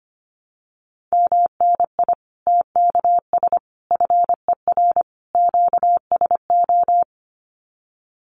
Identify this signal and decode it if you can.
Morse code, heard as a keyed tone, about 25 wpm, 700 Hz. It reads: MNI TKS FER QSO
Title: Common QSO Phrases 25wpm